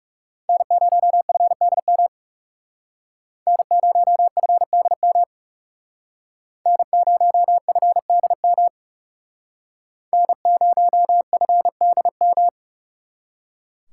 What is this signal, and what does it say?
Morse code, heard as a keyed tone, about 38 wpm, 700 Hz. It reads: N0FDM N0FDM N0FDM N0FDM